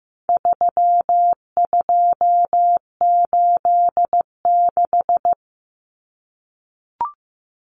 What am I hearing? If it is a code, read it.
Morse code, heard as a keyed tone, about 15 wpm, 700 Hz. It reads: 3286
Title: Common Cut Numbers - 4-Digit 15wpm